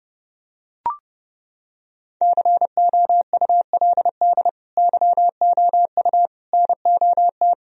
Morse code, heard as a keyed tone, about 30 wpm, 700 Hz. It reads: COULD YOU NOT
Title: Sentences from Top 1000 Words - Review Part 1 - 30wpm